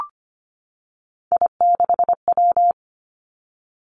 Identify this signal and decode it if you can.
Morse code, heard as a keyed tone, about 25 wpm, 700 Hz. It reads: I6W
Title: Three Challenging Letter-Number Combinations 25wpm